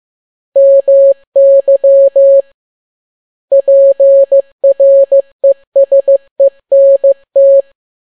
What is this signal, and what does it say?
Morse code, heard as a keyed tone, about 15 wpm, 550 Hz. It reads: MY PRESENT